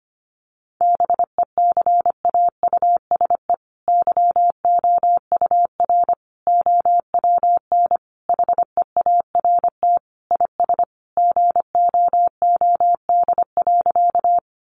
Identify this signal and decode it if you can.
Morse code, heard as a keyed tone, about 25 wpm, 700 Hz. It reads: BECAUSE YOUR OWN HEART IS GOOD.
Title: Sentences from Top 1200 Words 25wpm